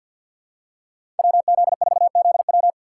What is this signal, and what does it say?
Morse code, heard as a keyed tone, about 52 wpm, 700 Hz. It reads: WZ4ZW
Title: USA Call Signs - 5 Characters - 2x Character Spacing - 50wpm